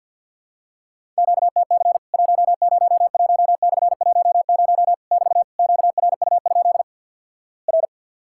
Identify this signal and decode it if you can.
Morse code, heard as a keyed tone, about 50 wpm, 700 Hz. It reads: QTC 101/10 <BT> QRU? R
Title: Worked All Europe DX Contest - QTC Exchange with Repeats - 50wpm